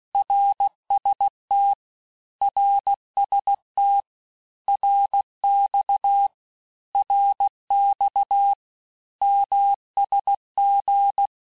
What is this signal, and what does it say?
Morse code, heard as a keyed tone, about 16 wpm, 800 Hz. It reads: RST RST RX RX MSG
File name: cours-du-lundi files / cours-1592--teste-abreviations-vitesse-15mpm-puis-20mpm